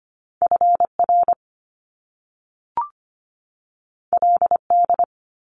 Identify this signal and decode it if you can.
Morse code, heard as a keyed tone, about 25 wpm, 700 Hz. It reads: FR LD
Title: Top Two Letter Combinations in Words 25wpm